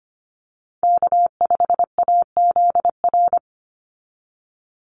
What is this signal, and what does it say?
Morse code, heard as a keyed tone, about 25 wpm, 700 Hz. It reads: K5AZR
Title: Call Signs - 25wpm